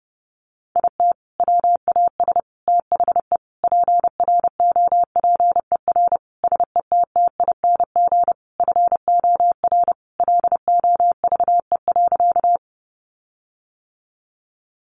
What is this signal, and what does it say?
Morse code, heard as a keyed tone, about 30 wpm, 700 Hz. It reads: IT WAS THE PROPER SETTING FOR LOVE.
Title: Sentences from Top 1300 Words 30wpm